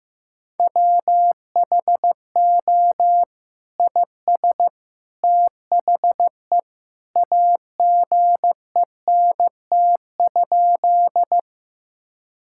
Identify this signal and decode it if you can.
Morse code, heard as a keyed tone, about 14 wpm, 700 Hz. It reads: WHO IS THE AGENT?
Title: Sentences from Top 1600 Words 15wpm